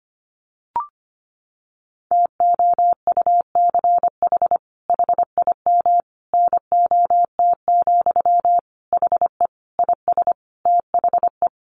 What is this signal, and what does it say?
Morse code, heard as a keyed tone, about 25 wpm, 700 Hz. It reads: TOUCH HIM NOT, HE IS THE
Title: Sentences from Top 1200 Words 25wpm